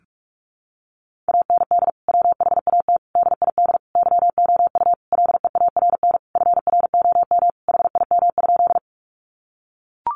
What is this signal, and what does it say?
Morse code, heard as a keyed tone, about 44 wpm, 700 Hz. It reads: AND WHAT DID YOU LEARN FROM HIM?
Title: Sentences from Top 300 Words 45wpm